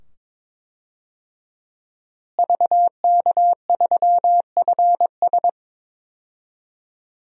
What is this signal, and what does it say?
Morse code, heard as a keyed tone, about 22 wpm, 700 Hz. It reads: VK3FS
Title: USA Call Signs - 5 Characters - Repeated 2x - 22wpm